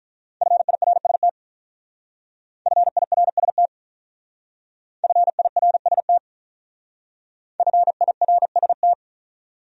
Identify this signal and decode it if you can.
Morse code, heard as a keyed tone, about 45 wpm, 700 Hz. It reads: FIRST FIRST FIRST FIRST